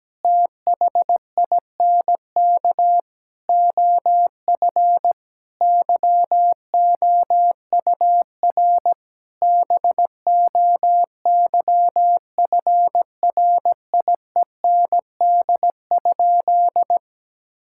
Morse code, heard as a keyed tone, about 17 wpm, 700 Hz. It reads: THINK OF YOUR BOYFRIEND?
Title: Sentences from Top 1000 Words - Review Part 5 - 17wpm